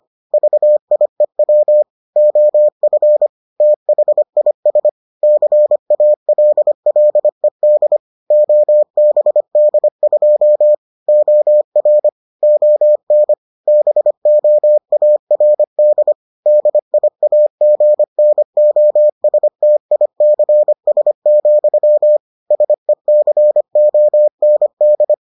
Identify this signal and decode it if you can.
Morse code, heard as a keyed tone, about 25 wpm, 600 Hz. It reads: VIEW OF THIS CALLED OBD2 OR ON BOARD DIAGNOSTICS, SECOND